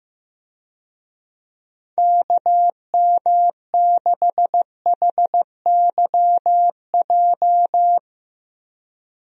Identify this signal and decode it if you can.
Morse code, heard as a keyed tone, about 16 wpm, 700 Hz. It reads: KM6HYJ
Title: USA Call Signs - 6 Characters - Repeated 3x - 15wpm